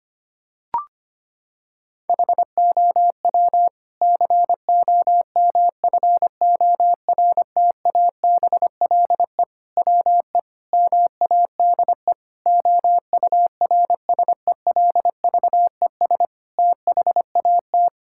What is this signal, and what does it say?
Morse code, heard as a keyed tone, about 25 wpm, 700 Hz. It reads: HOW COMFORTABLE WE MADE OURSELVES THAT
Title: Sentences from Top 1600 Words 25wpm